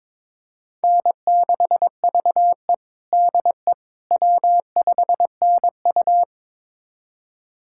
Morse code, heard as a keyed tone, about 22 wpm, 700 Hz. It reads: N6VE DE W5NU
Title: USA Callsign DE Callsign - 4 Characters - 22wpm